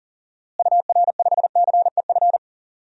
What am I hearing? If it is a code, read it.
Morse code, heard as a keyed tone, about 40 wpm, 700 Hz. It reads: UR5CEF
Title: DXCC-100 Call Signs - 6 Characters - Repeated 3x - 40wpm